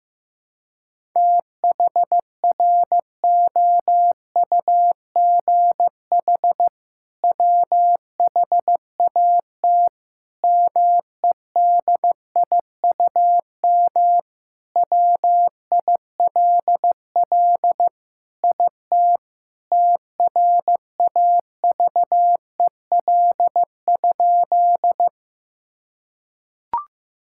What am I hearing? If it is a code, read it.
Morse code, heard as a keyed tone, about 15 wpm, 700 Hz. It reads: THROUGH WHAT MEDIUM WILL IT TRAVEL?
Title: Sentences from Top 1000 Words - Review Part 6 - 15wpm